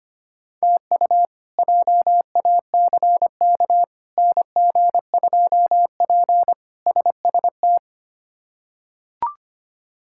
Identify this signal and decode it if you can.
Morse code, heard as a keyed tone, about 25 wpm, 700 Hz. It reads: TU JACK NG2P SST
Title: K1USN SST QSO Practice - 25wpm